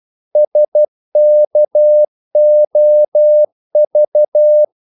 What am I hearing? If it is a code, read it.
Morse code, heard as a keyed tone, about 12 wpm, 600 Hz. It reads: SKOV
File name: Korte ord DK 12wpm